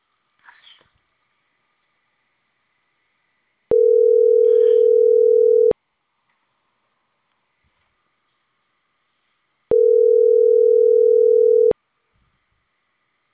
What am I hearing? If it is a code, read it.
Ringback tone (North America), 440 + 480 Hz.